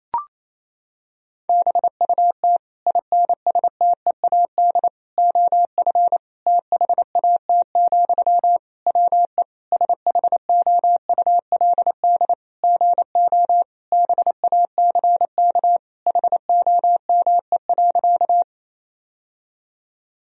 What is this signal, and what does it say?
Morse code, heard as a keyed tone, about 28 wpm, 700 Hz. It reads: BUT INSTEAD OF THAT, WE SHOULD GO BACK HOME.
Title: Sentences from Top 1000 Words - Review Part 4 - 28wpm